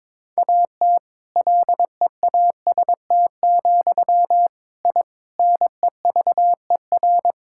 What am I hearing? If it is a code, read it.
Morse code, heard as a keyed tone, about 22 wpm, 700 Hz. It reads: AT LEAST, I NEVER
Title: Sentences from Top 1000 Words - Review Part 5 - 22wpm